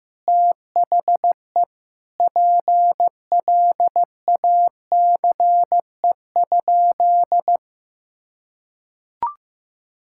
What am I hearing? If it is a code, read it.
Morse code, heard as a keyed tone, about 15 wpm, 700 Hz. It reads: THE PLACE?
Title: Sentences from Top 1600 Words 15wpm